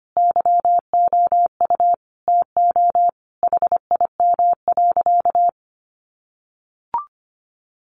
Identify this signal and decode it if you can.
Morse code, heard as a keyed tone, about 25 wpm, 700 Hz. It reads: YOU TO HIM.
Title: Sentences from Top 1200 Words 25wpm